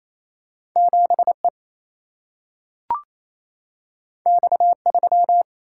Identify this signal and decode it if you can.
Morse code, heard as a keyed tone, about 28 wpm, 700 Hz. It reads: 7E X3
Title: Single-Letter Single-Number 28wpm